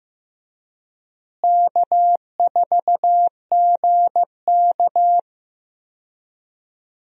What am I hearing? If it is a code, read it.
Morse code, heard as a keyed tone, about 14 wpm, 700 Hz. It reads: K4GK
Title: USA Call Signs - 4 Characters - Repeated 3x - 15wpm